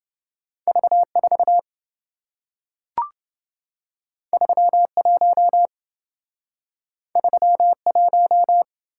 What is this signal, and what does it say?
Morse code, heard as a keyed tone, about 27 wpm, 700 Hz. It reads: V4 31 31